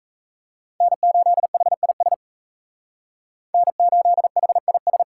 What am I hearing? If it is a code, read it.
Morse code, heard as a keyed tone, about 37 wpm, 700 Hz. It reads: N8HIS N8HIS